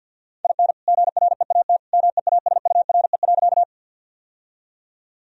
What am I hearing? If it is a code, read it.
Morse code, heard as a keyed tone, about 51 wpm, 700 Hz. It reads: IN GREAT MEASURE.
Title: Sentences from Top 1000 Words - Review Part 2 - 50wpm